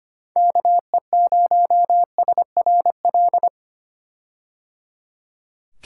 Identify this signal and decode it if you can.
Morse code, heard as a keyed tone, about 25 wpm, 700 Hz. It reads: KE0SRL
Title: K1USN SST QSO Practice - 25wpm